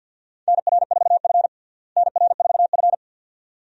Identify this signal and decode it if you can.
Morse code, heard as a keyed tone, about 52 wpm, 700 Hz. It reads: NR4F NR4F
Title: USA Call Signs - 4 Characters - Repeated 2x - 50wpm